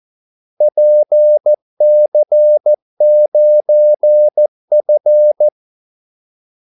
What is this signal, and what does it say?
Morse code, heard as a keyed tone, about 14 wpm, 600 Hz. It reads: PC9F
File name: Callsigns-14wpm